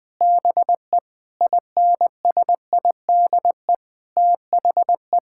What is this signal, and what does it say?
Morse code, heard as a keyed tone, about 20 wpm, 700 Hz. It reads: BE INSIDE THE
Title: Sentences from Top 1000 Words - Review Part 6 - 20wpm